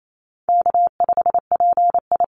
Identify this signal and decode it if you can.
Morse code, heard as a keyed tone, about 28 wpm, 700 Hz. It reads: K5PI
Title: Easy Call Signs Repeated 2x - 28wpm